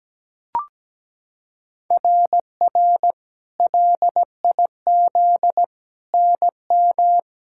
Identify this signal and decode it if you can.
Morse code, heard as a keyed tone, about 17 wpm, 700 Hz. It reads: RR LIZ NM
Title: SOTA QSO Practice 17wpm